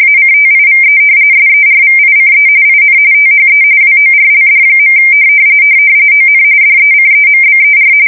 RTTY, about 45.45 baud. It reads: ACTIVE SERVICE IN THE WORLD. ON DECEMBER
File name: fsk170 / battleshipnj